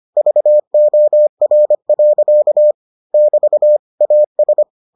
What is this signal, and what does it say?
Morse code, heard as a keyed tone, about 25 wpm, 600 Hz. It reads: VOR. <BT> AS